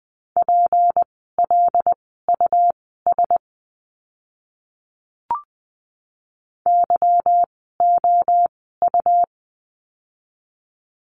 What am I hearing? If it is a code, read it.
Morse code, heard as a keyed tone, about 20 wpm, spaced as for 14 wpm, 700 Hz. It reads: PLUS YOU
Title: 153 Lesson 25f - Letter p - Words 20wpm